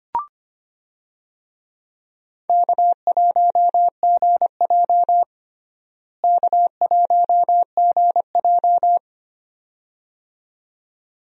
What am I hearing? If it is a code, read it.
Morse code, heard as a keyed tone, about 25 wpm, 700 Hz. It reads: K1GJ K1GJ
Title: USA Call Signs - 4 Characters - Repeated 2x - 25wpm